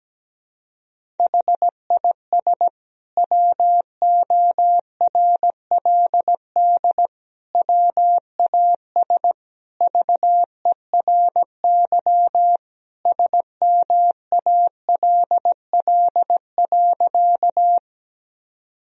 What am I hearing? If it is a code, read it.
Morse code, heard as a keyed tone, about 17 wpm, 700 Hz. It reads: HIS WORLD WAS VERY SMALL.
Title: Sentences from Top 1000 Words - Review Part 1 - 17wpm